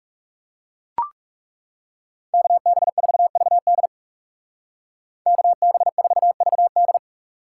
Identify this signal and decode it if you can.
Morse code, heard as a keyed tone, about 37 wpm, 700 Hz. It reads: KB4VD KB4VD
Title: Speed Racing - USA 2x2 Call Signs 30wpm